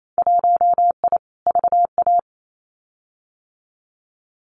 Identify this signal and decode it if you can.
Morse code, heard as a keyed tone, about 28 wpm, 700 Hz. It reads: 1I VA
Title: Winter Field Day Exchanges 28wpm